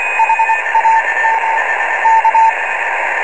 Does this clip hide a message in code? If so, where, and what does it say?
Yes: Morse code from 0.2 to 2.5 s, heard as a keyed tone, about 25 wpm, spaced as for 20 wpm, 900 Hz, with radio noise. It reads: HAM K